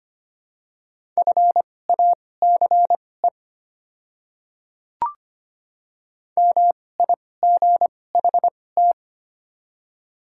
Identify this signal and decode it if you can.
Morse code, heard as a keyed tone, about 25 wpm, spaced as for 18 wpm, 700 Hz. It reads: FACE MIGHT
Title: 2x Character Spacing - Top 500 Words - 25wpm